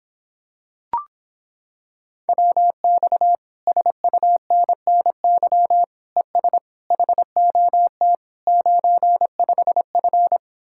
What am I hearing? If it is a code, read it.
Morse code, heard as a keyed tone, about 26 wpm, 700 Hz. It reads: WX SUNNY ES HOT 95F